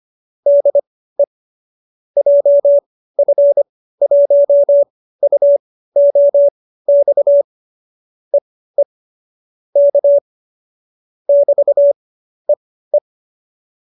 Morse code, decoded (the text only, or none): DE JF1UOX EE K <BT>EE